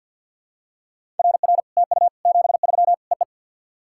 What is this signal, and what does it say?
Morse code, heard as a keyed tone, about 50 wpm, 700 Hz. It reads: RR TU 73 EE
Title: SOTA QSO Practice 50wpm